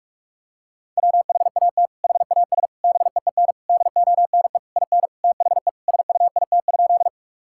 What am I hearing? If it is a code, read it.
Morse code, heard as a keyed tone, about 46 wpm, 700 Hz. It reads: WHAT HAS BEEN DONE IN THE SUIT?
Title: Sentences from Top 1600 Words 45wpm